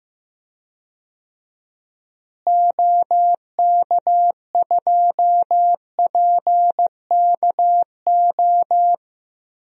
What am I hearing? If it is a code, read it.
Morse code, heard as a keyed tone, about 15 wpm, 700 Hz. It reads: OK2PKO
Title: DXCC-100 Call Signs - 6 Characters - Repeated 2x - 15wpm